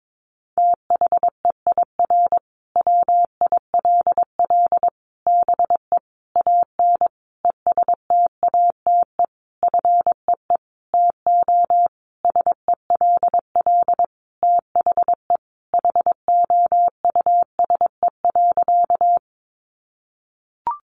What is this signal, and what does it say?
Morse code, heard as a keyed tone, about 22 wpm, 700 Hz. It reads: THEIR WILL BE AN ESTATE FEE TO SELL THE HOUSE.
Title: Sentences from Top 1500 Words 22wpm